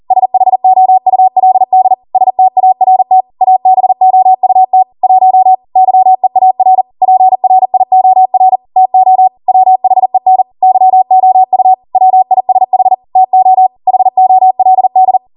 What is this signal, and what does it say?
Morse code, heard as a keyed tone, about 40 wpm, 750 Hz. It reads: SHOULD START ABOUT 1 YEAR PRIOR TO WHEN YOU WISH TO HOLD